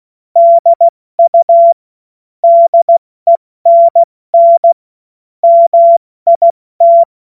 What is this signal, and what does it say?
Morse code, heard as a keyed tone, about 16 wpm, 680 Hz. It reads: DU DENN MIT
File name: Witz-29-Juli-2025-70-F80